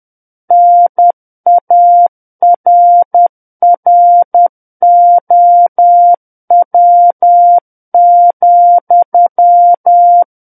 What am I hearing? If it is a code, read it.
Morse code, heard as a keyed tone, about 10 wpm, 700 Hz. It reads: NARROW,
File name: Book One / 01-Chapter 1 - 10 WPM